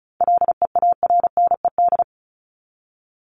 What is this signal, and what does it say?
Morse code, heard as a keyed tone, about 34 wpm, 700 Hz. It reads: LEARNED
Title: ED Suffix - Top 100 Words with Base Words 35wpm